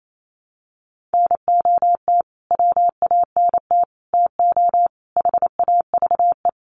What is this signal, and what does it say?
Morse code, heard as a keyed tone, about 28 wpm, 700 Hz. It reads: NOT WANT TO HAVE
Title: Sets of 5 Words Encore2 28wpm